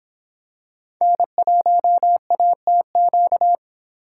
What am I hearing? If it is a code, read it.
Morse code, heard as a keyed tone, about 25 wpm, 700 Hz. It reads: N1ATQ